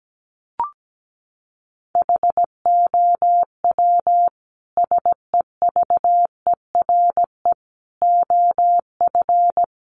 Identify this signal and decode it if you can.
Morse code, heard as a keyed tone, about 17 wpm, 700 Hz. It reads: HOW SEVERE OF
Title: Sentences from Top 1400 Words 17wpm